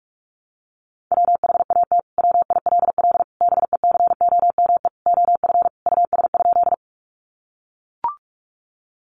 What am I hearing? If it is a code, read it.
Morse code, heard as a keyed tone, about 44 wpm, 700 Hz. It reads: WHAT WILL BECOME OF US?
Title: Sentences from Top 600 Words 45wpm